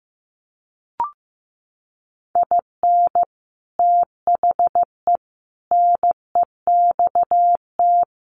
Morse code, heard as a keyed tone, about 15 wpm, 700 Hz. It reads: IN THE NEXT
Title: Sets of 5 Words Encore2 15wpm